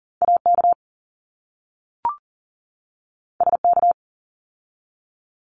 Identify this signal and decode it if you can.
Morse code, heard as a keyed tone, about 40 wpm, 700 Hz. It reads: AK SK
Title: ARRL Sections 40wpm